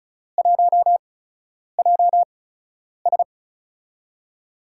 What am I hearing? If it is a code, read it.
Morse code, heard as a keyed tone, about 36 wpm, spaced as for 10 wpm, 700 Hz. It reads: 1JS
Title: USA Call Signs - 5 Characters - 8x Character Spacing - 35wpm